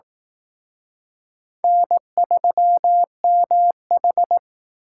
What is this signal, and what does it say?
Morse code, heard as a keyed tone, about 19 wpm, 700 Hz. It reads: N3MH
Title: Speed Racing - USA 1x2 Call Signs 15wpm